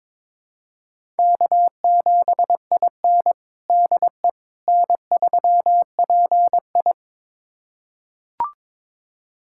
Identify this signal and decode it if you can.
Morse code, heard as a keyed tone, about 22 wpm, 700 Hz. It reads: K7IN DE N3PI
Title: USA Callsign DE Callsign - 4 Characters - 22wpm